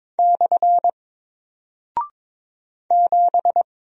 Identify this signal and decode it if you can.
Morse code, heard as a keyed tone, about 22 wpm, 700 Hz. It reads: / 7
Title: Single Character (Letter:Number:Punctuation:Prosign) Alt 22wpm